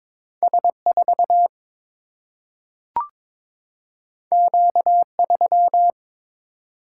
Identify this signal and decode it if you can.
Morse code, heard as a keyed tone, about 22 wpm, 700 Hz. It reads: S4 Q3
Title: Single-Letter Single-Number 22wpm